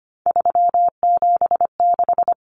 Morse code, heard as a keyed tone, about 25 wpm, 700 Hz. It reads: 376
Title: Common Cut Numbers - 3-Digit 25wpm